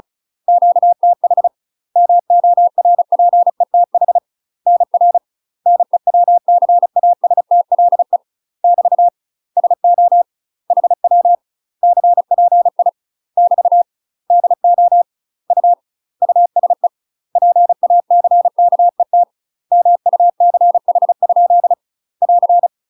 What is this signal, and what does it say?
Morse code, heard as a keyed tone, about 36 wpm, 700 Hz. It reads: QTH MORPETH NR NEWCASTLE <BT> SO HW CPI <BT> DO U USE PACKET MUCH? <AR>
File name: UK MASTER QSOs TRAINING / 35WPM / UK MASTER QSOs 0033